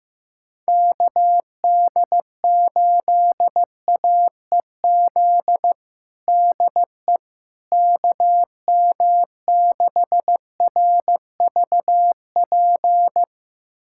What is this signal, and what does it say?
Morse code, heard as a keyed tone, about 15 wpm, 700 Hz. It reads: KD8AEZ DE KM6RVP
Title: USA Callsign DE Callsign - 6 Characters - 15wpm